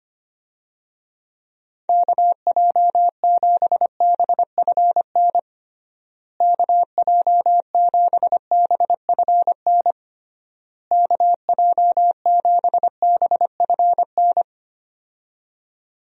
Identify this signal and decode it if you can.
Morse code, heard as a keyed tone, about 25 wpm, 700 Hz. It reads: KJ7BFN KJ7BFN KJ7BFN
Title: USA Call Signs - 6 Characters - Repeated 3x - 25wpm